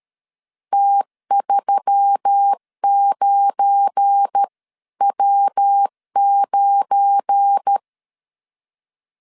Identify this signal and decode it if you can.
Morse code, heard as a keyed tone, about 13 wpm, 800 Hz. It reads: T39W9
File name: CW / DET / 1 9 1500C S B 12wpm 3819